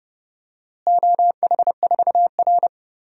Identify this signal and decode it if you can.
Morse code, heard as a keyed tone, about 30 wpm, 700 Hz. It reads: OH4R
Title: DXCC-100 Call Signs - 4 Characters - Repeated 3x - 30wpm